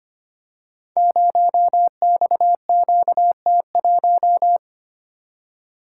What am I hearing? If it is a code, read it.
Morse code, heard as a keyed tone, about 25 wpm, 700 Hz. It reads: 0XQT1
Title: Code Groups - 5 Letter:Numbers - 25wpm